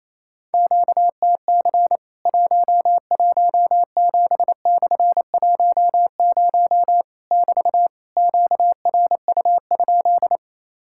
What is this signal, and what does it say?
Morse code, heard as a keyed tone, about 27 wpm, 700 Hz. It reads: QTC 117/10 <BT> QRU?
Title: Worked All Europe DX Contest - QTC Exchange - 28wpm